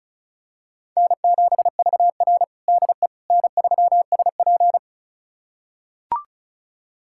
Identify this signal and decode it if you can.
Morse code, heard as a keyed tone, about 34 wpm, 700 Hz. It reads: N7VR DE N3SP
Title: USA Callsign DE Callsign - 4 Characters - 35wpm